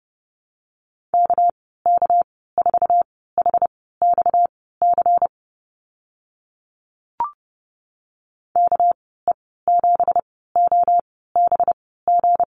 Morse code, heard as a keyed tone, about 29 wpm, spaced as for 17 wpm, 700 Hz. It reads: KK4HXC KE7OBG
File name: Lesson 37g - Letter x - Call Signs 30wpm